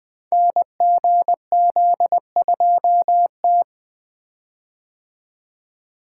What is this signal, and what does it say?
Morse code, heard as a keyed tone, about 20 wpm, 700 Hz. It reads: NGZ2T
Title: Code Groups - 5 Letter:Numbers - 20wpm